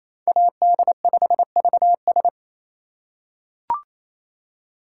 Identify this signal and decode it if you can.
Morse code, heard as a keyed tone, about 28 wpm, 700 Hz. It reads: AD5VS
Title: USA Call Signs - 4x Character Spacing - 28wpm